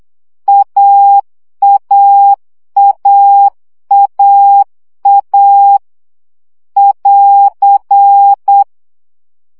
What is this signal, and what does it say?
Morse code, heard as a keyed tone, about 8 wpm, 800 Hz. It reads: AAAAA <AR>
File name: CW gyakorlas a 7 baud